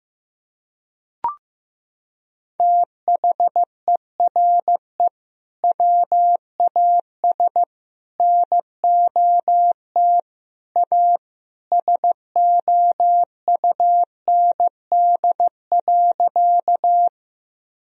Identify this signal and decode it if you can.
Morse code, heard as a keyed tone, about 16 wpm, 700 Hz. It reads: THERE WAS NOT A SOUND.
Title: Sentences from Top 1000 Words - Review Part 1 - 15wpm